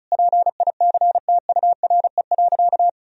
Morse code, heard as a keyed tone, about 35 wpm, 700 Hz. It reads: PICTURE.
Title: Sentences from Top 1000 Words - Review Part 1 - 35wpm